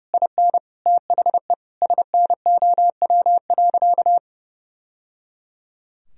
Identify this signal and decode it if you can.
Morse code, heard as a keyed tone, about 30 wpm, 700 Hz. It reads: IN THE SNOW.
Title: Sentences from Top 1000 Words - Review Part 3 - 30wpm